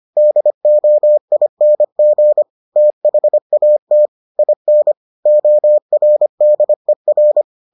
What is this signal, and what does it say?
Morse code, heard as a keyed tone, about 25 wpm, 600 Hz. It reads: DOING THAT IN ORDER